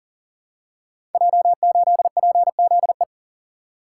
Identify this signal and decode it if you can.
Morse code, heard as a keyed tone, about 37 wpm, 700 Hz. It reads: J8PZE
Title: Code Groups - 5 Letter:Numbers - 40wpm